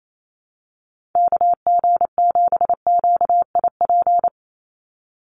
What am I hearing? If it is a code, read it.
Morse code, heard as a keyed tone, about 28 wpm, 700 Hz. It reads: KG7QIP
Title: USA Call Signs 28wpm